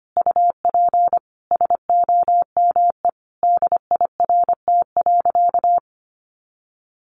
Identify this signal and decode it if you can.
Morse code, heard as a keyed tone, about 25 wpm, 700 Hz. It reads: UP SOME DIRT.
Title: A Sound Of Thunder 25wpm